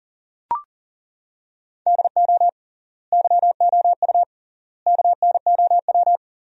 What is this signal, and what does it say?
Morse code, heard as a keyed tone, about 40 wpm, 700 Hz. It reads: DO YOU KNOW
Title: 3x Word-Spacing - Sentences from Top 300 Words - 40wpm